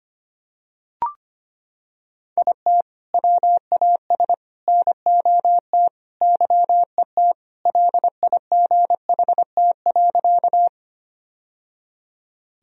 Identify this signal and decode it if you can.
Morse code, heard as a keyed tone, about 25 wpm, 700 Hz. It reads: IT WAS NOT YET LIGHT.
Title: Sentences from Top 1000 Words - Review Part 1 - 25wpm